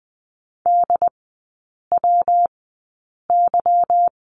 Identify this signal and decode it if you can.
Morse code, heard as a keyed tone, about 20 wpm, spaced as for 8 wpm, 700 Hz. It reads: DWY
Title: 152 Lesson 25e - Letter p - Three Characters 20wpm